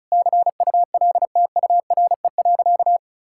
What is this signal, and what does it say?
Morse code, heard as a keyed tone, about 36 wpm, 700 Hz. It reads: CULTURE.
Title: Sentences from Top 1000 Words - Review Part 6 - 35wpm